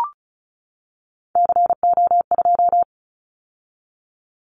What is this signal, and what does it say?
Morse code, heard as a keyed tone, about 36 wpm, 700 Hz. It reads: CO2
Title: Callsign Prefixes for Top 100 DXCC - 35wpm